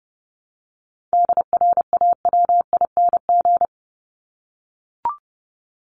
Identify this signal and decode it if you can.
Morse code, heard as a keyed tone, about 30 wpm, 700 Hz. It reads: DRAWING
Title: ING Suffix - Top 100 Words with Base Words 30wpm